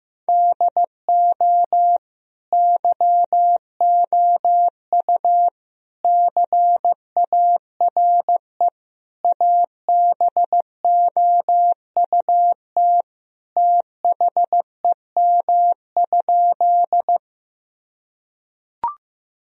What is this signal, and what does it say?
Morse code, heard as a keyed tone, about 15 wpm, 700 Hz. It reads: DO YOU CARE ABOUT THEM?
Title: Sentences from Top 1000 Words - Review Part 1 - 15wpm